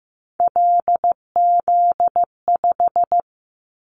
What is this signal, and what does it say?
Morse code, heard as a keyed tone, about 15 wpm, 700 Hz. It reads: LZ5
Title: Callsign Prefixes for Top 100 DXCC - 15wpm